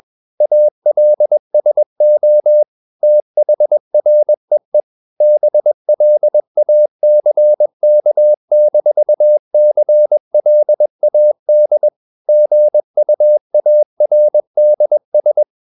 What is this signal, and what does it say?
Morse code, heard as a keyed tone, about 21 wpm, 600 Hz. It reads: ALSO THREE BLACK-CLAD GUARDS
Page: CW med Gnister